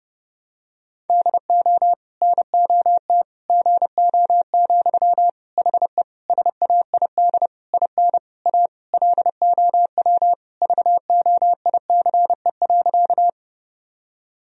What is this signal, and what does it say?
Morse code, heard as a keyed tone, about 30 wpm, 700 Hz. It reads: DO NOT GO, HE SAID IN A LOW VOICE.
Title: Sentences from Top 1000 Words - Review Part 1 - 30wpm